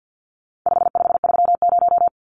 Morse code, heard as a keyed tone, about 50 wpm, 700 Hz. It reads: 5530
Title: Numbers - 4-Digit 50wpm